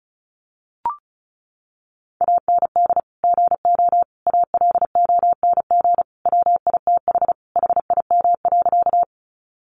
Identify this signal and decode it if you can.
Morse code, heard as a keyed tone, about 34 wpm, 700 Hz. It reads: AND GO ALONG WITH HIM.
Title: Sentences from Top 700 Words 35wpm